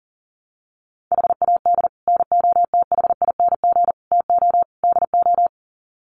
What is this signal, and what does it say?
Morse code, heard as a keyed tone, about 40 wpm, 700 Hz. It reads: HAD NOTHING TO DO
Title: Sets of 4 Words 40wpm